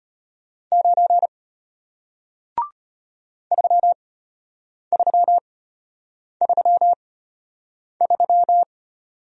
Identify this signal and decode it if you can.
Morse code, heard as a keyed tone, about 33 wpm, spaced as for 8 wpm, 700 Hz. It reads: 9 3333